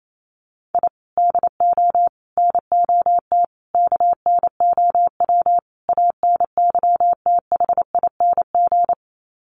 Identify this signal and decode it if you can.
Morse code, heard as a keyed tone, about 28 wpm, 700 Hz. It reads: I DO NOT KNOW ANYTHING
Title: Sets of 5 Words Encore 28wpm